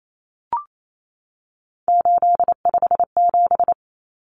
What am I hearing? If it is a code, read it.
Morse code, heard as a keyed tone, about 28 wpm, 700 Hz. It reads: 857
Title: Numbers - 3-Digit 28wpm